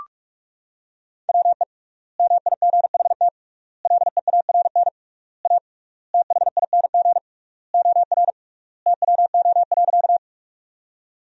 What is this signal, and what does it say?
Morse code, heard as a keyed tone, about 41 wpm, 700 Hz. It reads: WE MIGHT LEARN A THING OR TWO.
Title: 3x Word-Spacing - Sentences from Top 300 Words - 45wpm